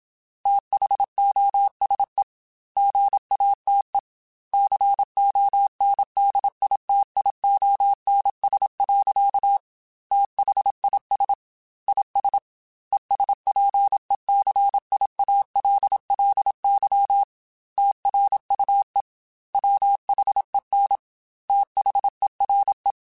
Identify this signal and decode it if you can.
Morse code, heard as a keyed tone, about 27 wpm, 800 Hz. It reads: THOSE GATE CONDITIONS. THIS IS ESPECIALLY TRUE WHEN THERE
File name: Technical Subjects / Technical Subjects-mp3s / AND Gate @25